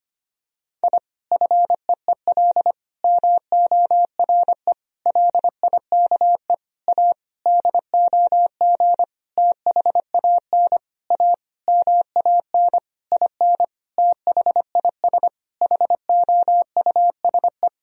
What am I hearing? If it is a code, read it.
Morse code, heard as a keyed tone, about 25 wpm, 700 Hz. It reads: I FEEL MORE LIKE A DOG THAN A MAN IN THIS HOUSE
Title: Sentences from Top 1000 Words - Review Part 2 - 25wpm